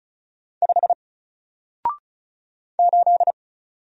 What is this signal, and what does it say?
Morse code, heard as a keyed tone, about 36 wpm, 700 Hz. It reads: 5 8
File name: Single-Letter-Number-35wpm